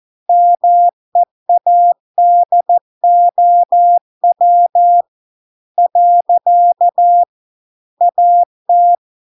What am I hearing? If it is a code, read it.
Morse code, heard as a keyed tone, about 14 wpm, 700 Hz. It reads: MEADOW . AT